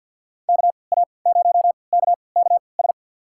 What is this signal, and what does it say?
Morse code, heard as a keyed tone, about 50 wpm, spaced as for 28 wpm, 700 Hz. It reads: KA0KKS
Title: USA Call Signs - 6 Characters - 3x Character Spacing - 50wpm